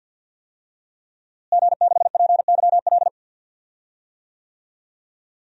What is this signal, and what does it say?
Morse code, heard as a keyed tone, about 52 wpm, 700 Hz. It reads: G6PYL
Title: DXCC-100 Call Signs - 5 Characters - Repeated 2x - 50wpm